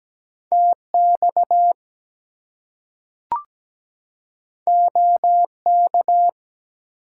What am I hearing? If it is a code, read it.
Morse code, heard as a keyed tone, about 17 wpm, 700 Hz. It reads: TX OK
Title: US States and Canadian Provinces 17wpm